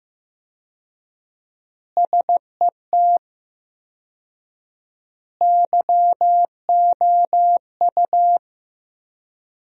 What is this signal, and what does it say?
Morse code, heard as a keyed tone, about 16 wpm, 700 Hz. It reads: SET YOU
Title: 4x Word-Spacing - Sentences from Top 200 Words - 15wpm